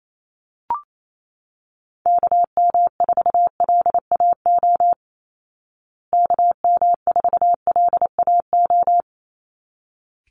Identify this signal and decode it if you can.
Morse code, heard as a keyed tone, about 28 wpm, 700 Hz. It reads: KM4LAO KM4LAO
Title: Call Signs Repeated 3x - 28wpm